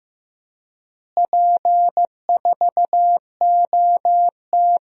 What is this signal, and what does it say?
Morse code, heard as a keyed tone, about 15 wpm, 700 Hz. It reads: P4OT
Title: DXCC-100 Call Signs - 4 Characters - Repeated 2x - 15wpm